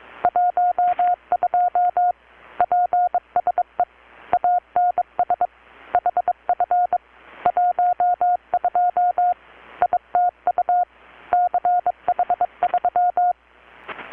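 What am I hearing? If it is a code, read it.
Morse code, heard as a keyed tone, about 22 wpm, 720 Hz. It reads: 12 PSE ANS HF 12 ITU CH3
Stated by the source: Morse from KPH's Night of Nights XX maritime radiotelegraphy event, received in Chiapas, Mexico about 3,800 km away on a Kenwood R-600 with a grounded 10 m long-wire antenna.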